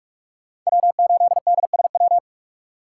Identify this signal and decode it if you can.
Morse code, heard as a keyed tone, about 43 wpm, 700 Hz. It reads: W8DSW
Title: USA Call Signs - 5 Characters - 2x Character Spacing - 45wpm